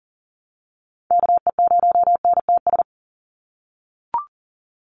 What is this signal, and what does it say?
Morse code, heard as a keyed tone, about 40 wpm, 700 Hz. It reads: KE0NTS
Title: USA Call Signs 40wpm